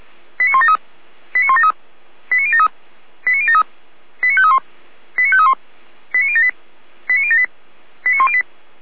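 Five-tone selective call (ZVEI), heard as quick runs of five tones, tones of about 70 ms. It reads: 89173 89173 89073 89073 89531 89531 89078 89078 89108
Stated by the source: ZVEI five-tone call received on an SDR.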